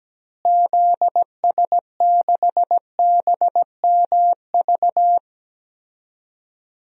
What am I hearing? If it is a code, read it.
Morse code, heard as a keyed tone, about 17 wpm, 700 Hz. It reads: ZS6BMV
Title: DXCC-100 Call Signs - 6 Characters - Repeated 3x - 17wpm